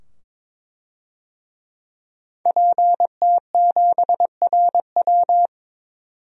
Morse code, heard as a keyed tone, about 22 wpm, 700 Hz. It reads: PT7RW
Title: DXCC-100 Call Signs - 5 Characters - 22wpm